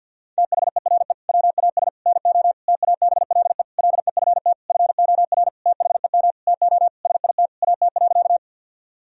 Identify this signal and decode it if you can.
Morse code, heard as a keyed tone, about 50 wpm, 700 Hz. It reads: THERE WAS NO TABLE LEFT FOR THEM TO SIT AT.
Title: Sentences from Top 1000 Words - Review Part 2 - 50wpm